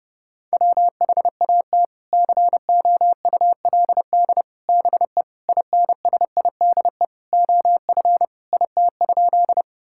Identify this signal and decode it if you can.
Morse code, heard as a keyed tone, about 32 wpm, 700 Hz. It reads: WHAT COULD BE INSIDE OF IT?
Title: Sentences from Top 1000 Words - Review Part 6 - 30wpm